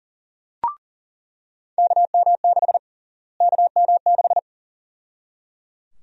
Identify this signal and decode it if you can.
Morse code, heard as a keyed tone, about 37 wpm, 700 Hz. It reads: KM6 KM6
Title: USA Call Sign Prefixes - Repeated 2x - 40wpm